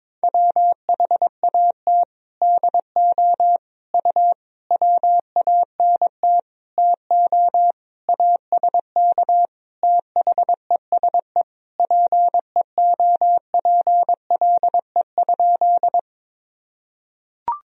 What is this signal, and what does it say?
Morse code, heard as a keyed tone, about 22 wpm, 700 Hz. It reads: WHAT DO U WANT TO ASK THESE PEOPLE?
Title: Sentences from Top 200 Words and Abbreviations 22wpm